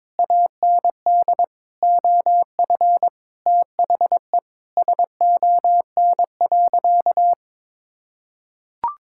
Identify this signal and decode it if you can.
Morse code, heard as a keyed tone, about 22 wpm, 700 Hz. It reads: AND OF THE SON.
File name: Sentences with 9-Word Phrases - 22wpm